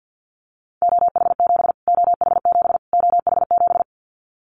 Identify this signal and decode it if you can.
Morse code, heard as a keyed tone, about 50 wpm, 700 Hz. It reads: OH7 OH7 OH7
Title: Callsign Prefixes for Top 25 DXCC - Repeated 3x - 50wpm